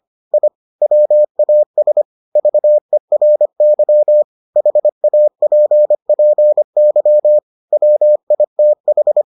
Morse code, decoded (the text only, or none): I WAS VERY HAPPY WITH